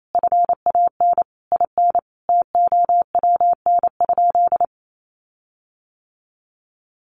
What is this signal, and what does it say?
Morse code, heard as a keyed tone, about 28 wpm, 700 Hz. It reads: FAN IN TOWN?
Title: Sentences from Top 1300 Words 28wpm